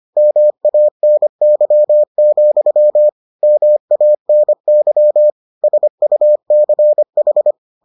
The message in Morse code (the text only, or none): MANY, MANY SUCH